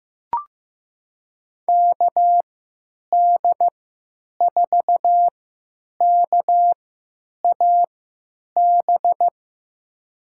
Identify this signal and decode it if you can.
Morse code, heard as a keyed tone, about 15 wpm, spaced as for 8 wpm, 700 Hz. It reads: KD4KAB
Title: USA Call Signs - 6 Characters - 3x Character Spacing - 15wpm